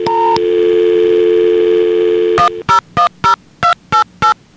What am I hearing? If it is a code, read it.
DTMF, heard as telephone touch tones, tones of about 120 ms. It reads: 1010288